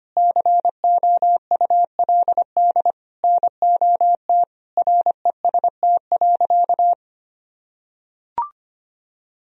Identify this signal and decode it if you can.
Morse code, heard as a keyed tone, about 25 wpm, 700 Hz. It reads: COULD NOT REST.
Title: Sentences from Top 1000 Words - Review Part 2 - 25wpm